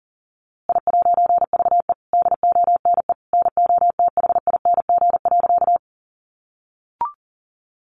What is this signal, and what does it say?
Morse code, heard as a keyed tone, about 40 wpm, 700 Hz. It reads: I'VE DONE NOTHING.
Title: A Sound Of Thunder 40wpm